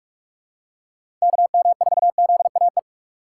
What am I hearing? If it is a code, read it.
Morse code, heard as a keyed tone, about 45 wpm, 700 Hz. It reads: KM4ZAE
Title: USA Call Signs - 6 Characters - 3x Character Spacing - 45wpm